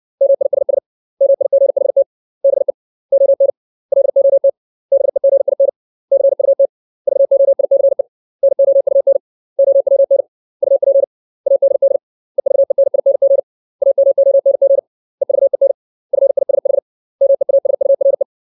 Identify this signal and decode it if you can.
Morse code, heard as a keyed tone, about 60 wpm, 550 Hz. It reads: MISS MIGHT BE ON LOT BEGIN CUT VOICE NOUN OWN UP ADD EVENING AMONG EVEN FISH MEASURE
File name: 632 words / 632 Words 60WPM